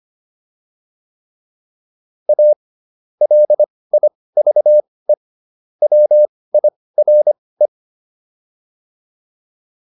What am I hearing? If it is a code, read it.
Morse code, heard as a keyed tone, about 25 wpm, spaced as for 18 wpm, 600 Hz. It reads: A LIVE WIRE
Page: intermediate-practice-files